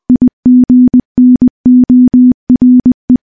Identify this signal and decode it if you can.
Morse code, heard as a keyed tone, about 21 wpm, 250 Hz. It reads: IGNORE
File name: morse-training-20WPM